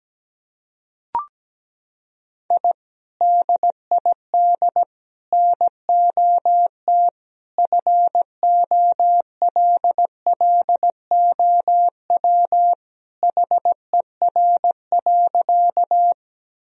Morse code, heard as a keyed tone, about 17 wpm, 700 Hz. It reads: I DID NOT FOLLOW HER.
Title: Sentences from Top 1000 Words - Review Part 1 - 17wpm